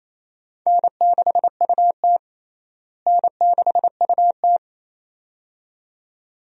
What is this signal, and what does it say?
Morse code, heard as a keyed tone, about 28 wpm, 700 Hz. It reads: N6UT N6UT
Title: USA Call Signs - 4 Characters - Repeated 2x - 28wpm